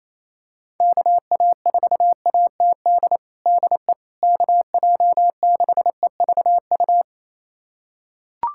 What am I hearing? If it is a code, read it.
Morse code, heard as a keyed tone, about 28 wpm, 700 Hz. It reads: KA4ATD DE KJ6EVU
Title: USA Callsign DE Callsign - 6 Characters - 28wpm